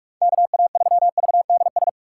Morse code, heard as a keyed tone, about 47 wpm, 700 Hz. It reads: KA3VDS
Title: USA Call Signs - 6 Characters - 3x Character Spacing - 45wpm